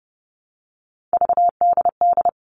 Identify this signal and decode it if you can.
Morse code, heard as a keyed tone, about 30 wpm, 700 Hz. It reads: VDD
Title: All Cut Numbers - 3-Digit 30wpm